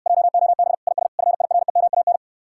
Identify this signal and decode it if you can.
Morse code, heard as a keyed tone, about 70 wpm, 700 Hz. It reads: JOB IS FIREMAN